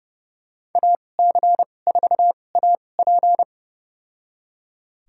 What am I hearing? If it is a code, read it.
Morse code, heard as a keyed tone, about 30 wpm, spaced as for 22 wpm, 700 Hz. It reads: AC4AP
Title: USA Call Signs - 5 Characters - 2x Character Spacing - 30wpm